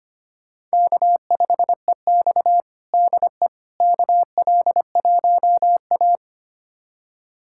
Morse code, heard as a keyed tone, about 25 wpm, 700 Hz. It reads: K5EX DE KL1A
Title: USA Callsign DE Callsign - 4 Characters - 25wpm